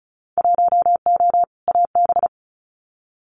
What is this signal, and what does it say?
Morse code, heard as a keyed tone, about 36 wpm, 700 Hz. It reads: 1O AB
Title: Winter Field Day Exchanges 35wpm